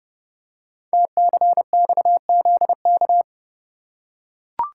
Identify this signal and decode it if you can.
Morse code, heard as a keyed tone, about 28 wpm, 700 Hz. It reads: TCXZK
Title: Code Groups - 5 Letters - 30wpm